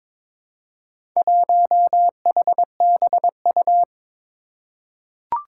Morse code, heard as a keyed tone, about 22 wpm, 700 Hz. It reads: 1HBU
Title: Four Challenging Letter-Number Combinations 22wpm